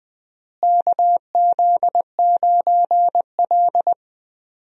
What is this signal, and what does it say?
Morse code, heard as a keyed tone, about 21 wpm, 700 Hz. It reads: KZ9L
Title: USA Call Signs - 4 Characters - Repeated 2x - 20wpm